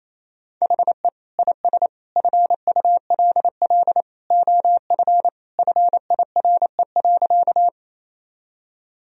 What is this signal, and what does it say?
Morse code, heard as a keyed tone, about 28 wpm, 700 Hz. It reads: HE IS FULL OF FIRE.
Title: Sentences from Top 1000 Words - Review Part 3 - 28wpm